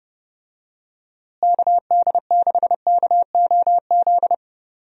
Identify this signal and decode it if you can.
Morse code, heard as a keyed tone, about 30 wpm, 700 Hz. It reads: KD6KOZ
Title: USA Call Signs - 6 Characters - Repeated 2x - 30wpm